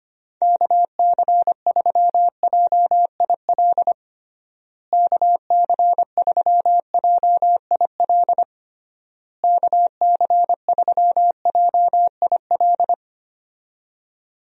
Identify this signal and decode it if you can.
Morse code, heard as a keyed tone, about 25 wpm, 700 Hz. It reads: KC3JIL KC3JIL KC3JIL
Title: USA Call Signs - 6 Characters - Repeated 3x - 25wpm